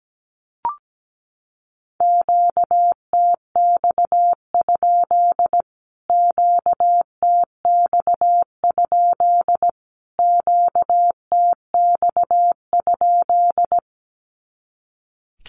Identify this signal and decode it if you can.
Morse code, heard as a keyed tone, about 17 wpm, 700 Hz. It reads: QTX? QTX? QTX?
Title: Q-Codes Repeated 3x 17wpm